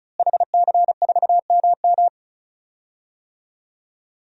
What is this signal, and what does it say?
Morse code, heard as a keyed tone, about 34 wpm, 700 Hz. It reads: HC4MM
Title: DXCC-100 Call Signs - 5 Characters - Repeated 2x - 35wpm